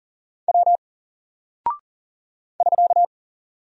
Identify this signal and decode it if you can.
Morse code, heard as a keyed tone, about 40 wpm, 700 Hz. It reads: W <SK>
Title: Single Character (Letter:Number:Punctuation:Prosign) 40wpm